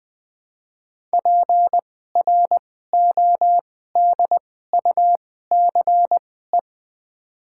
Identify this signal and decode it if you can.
Morse code, heard as a keyed tone, about 20 wpm, spaced as for 15 wpm, 700 Hz. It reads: PRODUCE
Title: 2x Character Spacing - Top 500 Words - 20wpm